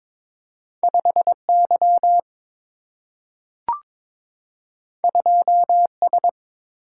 Morse code, heard as a keyed tone, about 22 wpm, 700 Hz. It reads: 5 Y 2 S
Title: Single-Letter Single-Number 22wpm